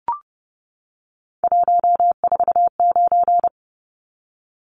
Morse code, heard as a keyed tone, about 30 wpm, 700 Hz. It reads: 149
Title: Numbers - 3-Digit 30wpm